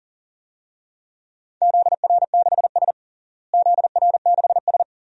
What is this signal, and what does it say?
Morse code, heard as a keyed tone, about 40 wpm, 700 Hz. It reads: ZR6S ZR6S
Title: DXCC-100 Call Signs - 4 Characters - Repeated 2x - 40wpm